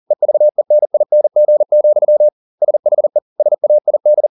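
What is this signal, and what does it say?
Morse code, heard as a keyed tone, about 40 wpm, 600 Hz. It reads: EVENING, SHE SAID
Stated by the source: Morse code at 40 wpm, practice text, CW med Gnister.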